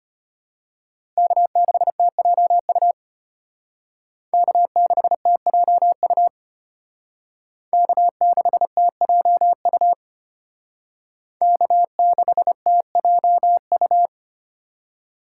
Morse code, heard as a keyed tone, about 32 wpm, 700 Hz. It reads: K6TJU K6TJU K6TJU K6TJU